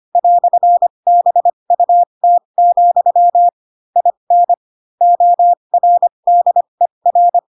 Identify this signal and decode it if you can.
Morse code, heard as a keyed tone, about 25 wpm, spaced as for 22 wpm, 700 Hz. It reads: "BUT, IN ORDER